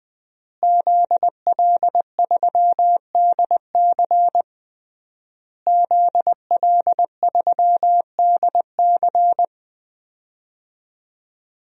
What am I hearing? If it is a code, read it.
Morse code, heard as a keyed tone, about 21 wpm, 700 Hz. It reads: ZL3DC ZL3DC
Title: DXCC-100 Call Signs - 5 Characters - Repeated 3x - 20wpm